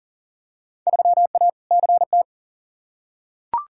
Morse code, heard as a keyed tone, about 40 wpm, 700 Hz. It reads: 3A CT
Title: Field Day Exchanges - 40wpm